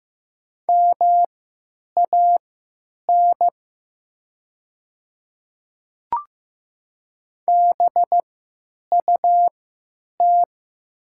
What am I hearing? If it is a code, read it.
Morse code, heard as a keyed tone, about 15 wpm, spaced as for 9 wpm, 700 Hz. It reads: MAN BUT
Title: 3x Character Spacing - Top 300 Words - 15wpm